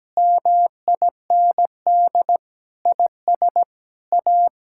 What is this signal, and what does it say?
Morse code, heard as a keyed tone, about 17 wpm, 700 Hz. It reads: MIND IS A
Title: Sentences from Top 1000 Words - Review Part 5 - 17wpm